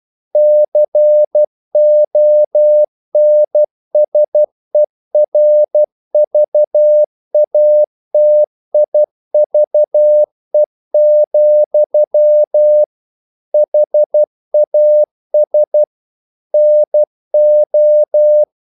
Morse code, heard as a keyed tone, about 12 wpm, 600 Hz. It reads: CONSERVATIVE, HAS NO
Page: English quotations in CW (telegraphy - morse)